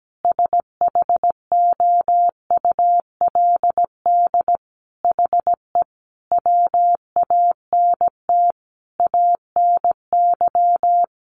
Morse code, heard as a keyed tone, about 17 wpm, 700 Hz. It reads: SHOULD HE WANT ANY
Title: Sentences from Top 1200 Words 17wpm